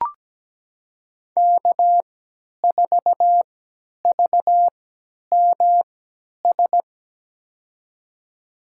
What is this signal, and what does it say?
Morse code, heard as a keyed tone, about 17 wpm, spaced as for 10 wpm, 700 Hz. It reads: K4VMS
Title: USA Call Signs - 5 Characters - 3x Character Spacing - 17wpm